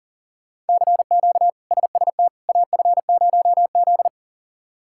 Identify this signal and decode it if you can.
Morse code, heard as a keyed tone, about 40 wpm, 700 Hz. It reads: CQ SST AF0Z
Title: K1USN SST QSO Practice - 40wpm